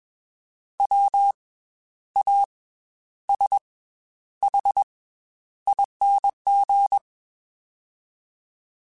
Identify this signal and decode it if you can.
Morse code, heard as a keyed tone, about 21 wpm, 800 Hz. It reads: W A S H ING